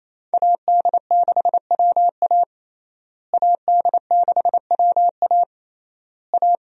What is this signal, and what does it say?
Morse code, heard as a keyed tone, about 28 wpm, 700 Hz. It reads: AD6WA AD6WA A
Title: USA Call Signs - 5 Characters - Repeated 3x - 28wpm